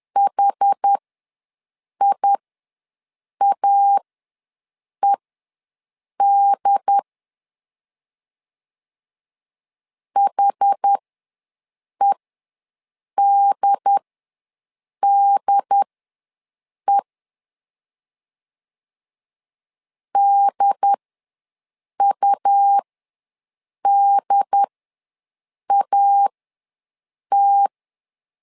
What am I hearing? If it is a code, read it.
Morse code, heard as a keyed tone, about 10 wpm, spaced as for 6 wpm, 800 Hz. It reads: HIAED HEDDE DUDAT
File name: LEZIONI / 2 UHDN / 2 UHD ws 10wpm 2058